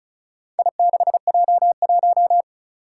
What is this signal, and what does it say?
Morse code, heard as a keyed tone, about 35 wpm, 700 Hz. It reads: I6J1
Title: Four Challenging Letter-Number Combinations 35wpm